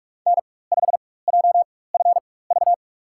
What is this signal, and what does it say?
Morse code, heard as a keyed tone, about 45 wpm, spaced as for 21 wpm, 700 Hz. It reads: N5JFV
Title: USA Call Signs - 4x Character Spacing - 45wpm